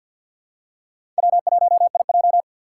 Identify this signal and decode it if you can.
Morse code, heard as a keyed tone, about 52 wpm, 700 Hz. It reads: W1IJ
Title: USA Call Signs - 4 Characters - 3x Character Spacing - 50wpm